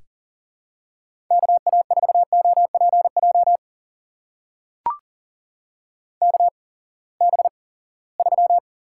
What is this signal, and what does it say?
Morse code, heard as a keyed tone, about 40 wpm, spaced as for 23 wpm, 700 Hz. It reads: KA4OPJ KB3
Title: USA Call Signs - 8x Character Spacing - 40wpm